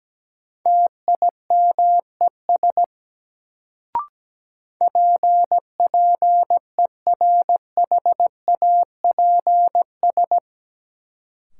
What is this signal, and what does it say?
Morse code, heard as a keyed tone, about 17 wpm, 700 Hz. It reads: TIMES PPERHAPS
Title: Sentences from Top 1100 Words - New Word Drill Alt - 17wpm